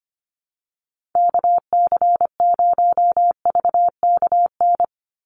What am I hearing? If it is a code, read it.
Morse code, heard as a keyed tone, about 25 wpm, 700 Hz. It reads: KC0VKN
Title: Call Signs Repeated 2x - 25wpm